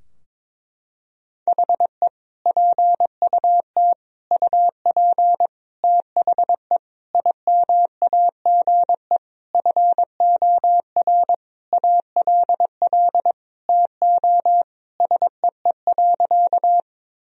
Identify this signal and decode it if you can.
Morse code, heard as a keyed tone, about 22 wpm, 700 Hz. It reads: HE PUT UP THE IMAGE FOR ALL TO SEE.
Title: Sentences from Top 1000 Words - Review Part 4 - 22wpm